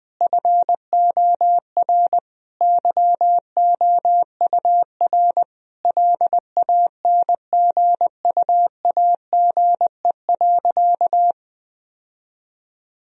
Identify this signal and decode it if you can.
Morse code, heard as a keyed tone, about 21 wpm, 700 Hz. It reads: FOR YOUR LANGUAGE.
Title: Sentences from Top 1000 Words - Review Part 3 - 20wpm